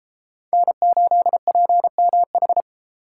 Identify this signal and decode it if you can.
Morse code, heard as a keyed tone, about 31 wpm, 700 Hz. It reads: N8PMH